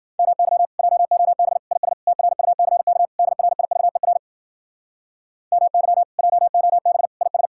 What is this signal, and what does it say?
Morse code, heard as a keyed tone, about 53 wpm, 700 Hz. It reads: MY JOB IS TRUCK DRIVER MY JOB IS